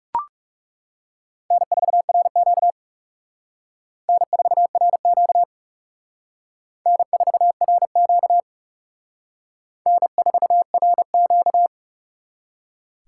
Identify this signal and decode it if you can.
Morse code, heard as a keyed tone, about 37 wpm, 700 Hz. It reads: N4RQ N4RQ N4RQ N4RQ
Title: Speed Racing - USA 1x2 Call Signs 30wpm